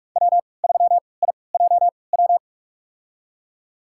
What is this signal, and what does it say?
Morse code, heard as a keyed tone, about 41 wpm, spaced as for 25 wpm, 700 Hz. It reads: W3IJW
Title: USA Call Signs - 5 Characters - 3x Character Spacing - 45wpm